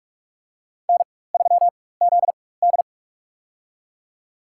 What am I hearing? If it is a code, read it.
Morse code, heard as a keyed tone, about 44 wpm, spaced as for 21 wpm, 700 Hz. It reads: N3ZD
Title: USA Call Signs - 4x Character Spacing - 45wpm